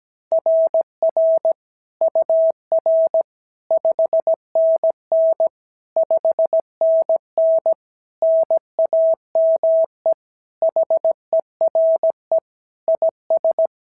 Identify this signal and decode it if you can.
Morse code, heard as a keyed tone, about 16 wpm, 650 Hz. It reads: RR UR 5NN 5NN NAME HERE IS
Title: SOTA QSO Practice 17wpm